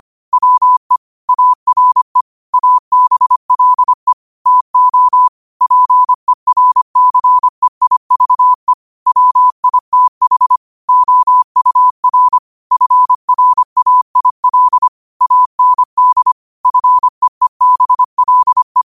Morse code, heard as a keyed tone, about 25 wpm, 1000 Hz. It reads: WE ARE ABLE TO PERCEIVE WITH OUR FRAIL AND FEEBLE
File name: qotdfiles / 25wpm